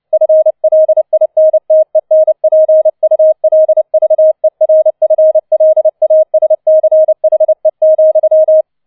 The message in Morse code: FLINTENPULVERFLASCHE,

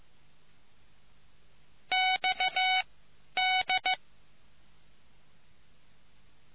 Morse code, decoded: X D

